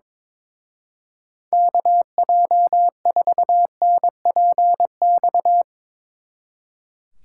Morse code, heard as a keyed tone, about 23 wpm, 700 Hz. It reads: KJ4NPX
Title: Speed Racing - USA 2x3 Call Signs 22wpm